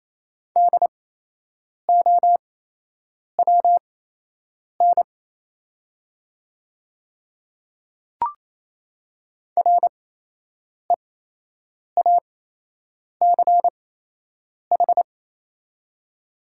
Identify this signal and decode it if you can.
Morse code, heard as a keyed tone, about 28 wpm, spaced as for 8 wpm, 700 Hz. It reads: DOWN REACH
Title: 8x Character Spacing - Top 500 Words - 28wpm